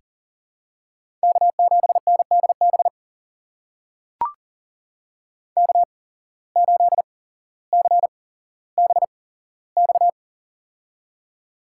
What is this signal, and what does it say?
Morse code, heard as a keyed tone, about 40 wpm, spaced as for 17 wpm, 700 Hz. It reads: K7NDB K8CBX
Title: USA Call Signs - 8x Character Spacing - 40wpm